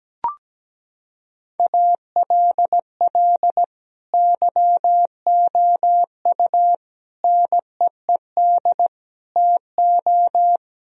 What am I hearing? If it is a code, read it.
Morse code, heard as a keyed tone, about 17 wpm, 700 Hz. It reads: ALL YOU NEED TO
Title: Sentences from Top 1000 Words - Review Part 4 - 17wpm